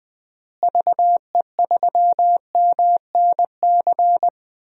Morse code, heard as a keyed tone, about 20 wpm, 700 Hz. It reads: VE3MNC
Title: DXCC-100 Call Signs - 6 Characters - Repeated 3x - 20wpm